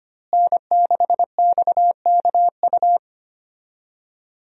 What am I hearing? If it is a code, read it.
Morse code, heard as a keyed tone, about 25 wpm, 700 Hz. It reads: N6XKU
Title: USA Call Signs - 5 Characters - Repeated 3x - 25wpm